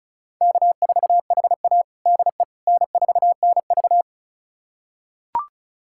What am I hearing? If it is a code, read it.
Morse code, heard as a keyed tone, about 34 wpm, 700 Hz. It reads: K4HA DE N4NV
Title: USA Callsign DE Callsign - 4 Characters - 35wpm